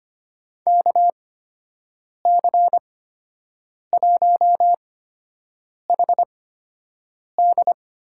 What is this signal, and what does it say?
Morse code, heard as a keyed tone, about 25 wpm, spaced as for 7 wpm, 700 Hz. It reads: KC1HD
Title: USA Call Signs - 6 Characters - 8x Character Spacing - 25wpm